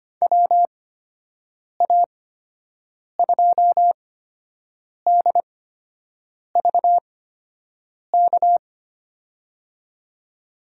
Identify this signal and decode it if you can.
Morse code, heard as a keyed tone, about 25 wpm, spaced as for 7 wpm, 700 Hz. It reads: WA2DVK
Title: USA Call Signs - 6 Characters - 8x Character Spacing - 25wpm